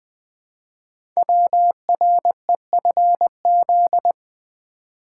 Morse code, heard as a keyed tone, about 20 wpm, 700 Hz. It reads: WREFZ
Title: Code Groups - 5 Letter:Numbers - 20wpm